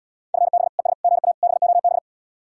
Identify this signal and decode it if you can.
Morse code, heard as a keyed tone, about 77 wpm, 700 Hz. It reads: FB ES GA BOB